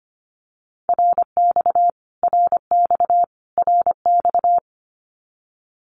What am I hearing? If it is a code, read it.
Morse code, heard as a keyed tone, about 25 wpm, 700 Hz. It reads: RX RX RX
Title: Abbreviations Repeated 3x - 25wpm